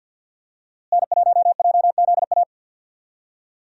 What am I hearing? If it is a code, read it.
Morse code, heard as a keyed tone, about 47 wpm, 700 Hz. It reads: N1JZA
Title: USA Call Signs - 5 Characters - Repeated 2x - 50wpm